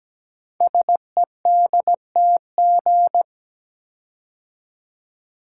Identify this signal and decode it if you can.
Morse code, heard as a keyed tone, about 17 wpm, 700 Hz. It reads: SEDTG
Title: Code Groups - 5 Letter:Numbers - 17wpm